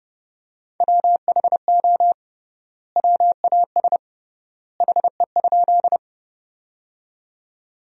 Morse code, heard as a keyed tone, about 30 wpm, 700 Hz. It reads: WHO WAS HE?
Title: 3x Word-Spacing - Sentences from Top 100 Words - 30wpm